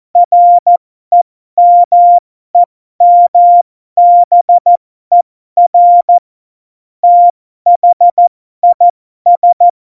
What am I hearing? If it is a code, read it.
Morse code, heard as a keyed tone, about 14 wpm, spaced as for 12 wpm, 700 Hz. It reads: REMEMBER THIS
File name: Morsekurs / Witze